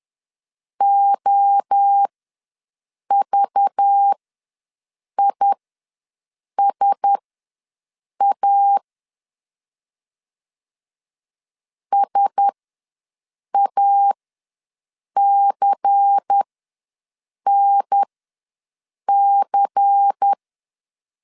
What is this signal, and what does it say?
Morse code, heard as a keyed tone, about 10 wpm, spaced as for 6 wpm, 800 Hz. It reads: OVISA SACNC
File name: LEZIONI / 5 LBVC / 5 LBVC ws 10wpm 3733